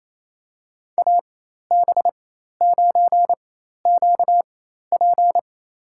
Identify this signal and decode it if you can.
Morse code, heard as a keyed tone, about 28 wpm, spaced as for 13 wpm, 700 Hz. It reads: AB9QP
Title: USA Call Signs - 5 Characters - 4x Character Spacing - 28wpm